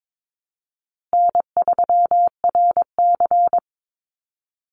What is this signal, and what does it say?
Morse code, heard as a keyed tone, about 22 wpm, 700 Hz. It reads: N3RC
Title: Call Signs Repeated 3x - 22wpm